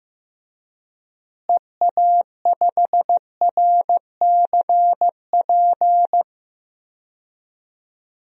Morse code, heard as a keyed tone, about 15 wpm, 700 Hz. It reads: EA5RCP
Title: DXCC-100 Call Signs - 6 Characters - Repeated 3x - 15wpm